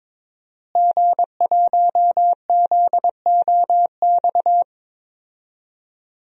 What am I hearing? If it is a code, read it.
Morse code, heard as a keyed tone, about 22 wpm, 700 Hz. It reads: G1ZOX
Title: DXCC-100 Call Signs - 5 Characters - Repeated 3x - 22wpm